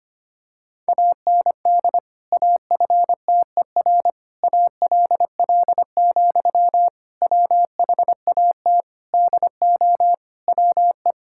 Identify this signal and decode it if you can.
Morse code, heard as a keyed tone, about 25 wpm, 700 Hz. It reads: AND AFTER ALL, WHAT DO WE
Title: Sentences from Top 1600 Words 25wpm